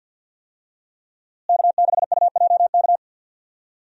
Morse code, heard as a keyed tone, about 51 wpm, 700 Hz. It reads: K6UJK
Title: USA Call Signs - 5 Characters - Repeated 2x - 50wpm